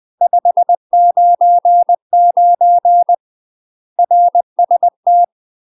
Morse code, heard as a keyed tone, about 20 wpm, 700 Hz. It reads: 599 RST
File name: UK MASTER QSOs TRAINING / 20WPM / UK MASTER QSOs 0072